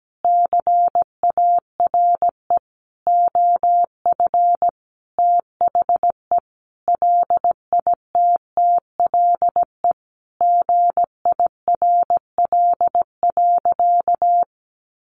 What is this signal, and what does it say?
Morse code, heard as a keyed tone, about 17 wpm, 700 Hz. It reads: CARE OF THE LITTLE GIRL.
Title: Sentences from Top 400 Words 17wpm